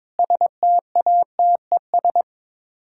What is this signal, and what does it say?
Morse code, heard as a keyed tone, about 22 wpm, 700 Hz. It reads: STATES
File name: practice / 22wpm / Sets of 6 Words - 22wpm